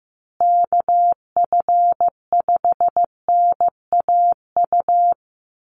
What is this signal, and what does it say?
Morse code, heard as a keyed tone, about 15 wpm, 700 Hz. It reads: KF5NAU
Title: USA Call Signs Repeated 2x 15wpm